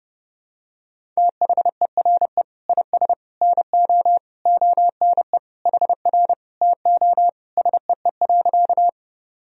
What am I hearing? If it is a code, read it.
Morse code, heard as a keyed tone, about 32 wpm, 700 Hz. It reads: THERE IS NO ONE HR TO SEE.
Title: Sentences from Top 200 Words and Abbreviations 30wpm